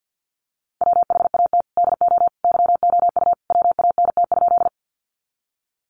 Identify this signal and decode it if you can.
Morse code, heard as a keyed tone, about 48 wpm, 700 Hz. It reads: WHAT DO YOU WANT?
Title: Sentences from Top 200 Words 50wpm